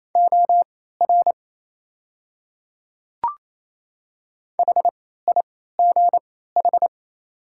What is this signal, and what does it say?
Morse code, heard as a keyed tone, about 28 wpm, spaced as for 16 wpm, 700 Hz. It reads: OR HIGH
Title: 3x Character Spacing - Top 200 Words - 28wpm